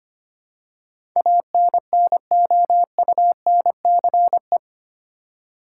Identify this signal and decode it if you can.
Morse code, heard as a keyed tone, about 25 wpm, 700 Hz. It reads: ANNOUNCE
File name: 8 Letter Words - First 3 Letters Given - 25wpm